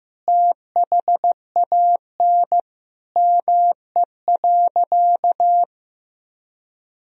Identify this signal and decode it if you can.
Morse code, heard as a keyed tone, about 15 wpm, 700 Hz. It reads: THAN ME.
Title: Sentences from Top 400 Words and Abbreviations 15wpm